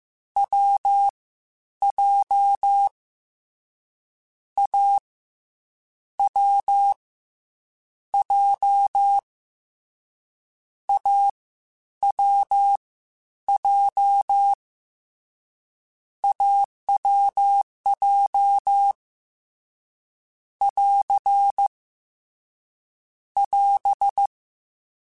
Morse code, heard as a keyed tone, about 15 wpm, 800 Hz. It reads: W J A W J A W J AWJ <AR> <AS>
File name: Difficult Combos / Difficult Combos-mp3s / AWJ1 @13